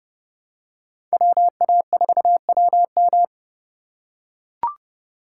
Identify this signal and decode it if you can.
Morse code, heard as a keyed tone, about 30 wpm, 700 Hz. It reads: WA4WM